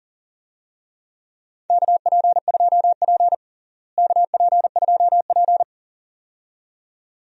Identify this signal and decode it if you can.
Morse code, heard as a keyed tone, about 37 wpm, 700 Hz. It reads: KP2P KP2P
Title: DXCC-100 Call Signs - 4 Characters - Repeated 2x - 40wpm